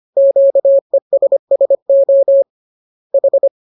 Morse code, heard as a keyed tone, about 25 wpm, 550 Hz. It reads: QESSO H